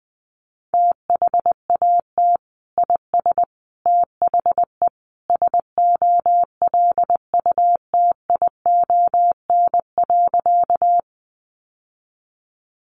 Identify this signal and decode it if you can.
Morse code, heard as a keyed tone, about 20 wpm, 700 Hz. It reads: THAT IS THE SOLUTION.
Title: Sentences from Top 1300 Words 20wpm